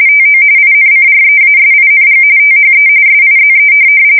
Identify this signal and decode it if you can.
RTTY, about 45.45 baud. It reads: TO INCREASING THE SIZE
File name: fsk170 / USS-Massachusetts